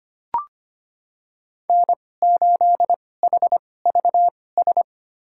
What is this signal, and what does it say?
Morse code, heard as a keyed tone, about 25 wpm, spaced as for 18 wpm, 700 Hz. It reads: N8HVS
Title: USA Call Signs - 5 Characters - 2x Character Spacing - 25wpm